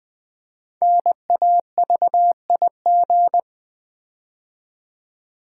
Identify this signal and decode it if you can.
Morse code, heard as a keyed tone, about 20 wpm, 700 Hz. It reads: NAVIG
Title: Build A Word - 5 Letters At A Time - 10-Letter Words - 20wpm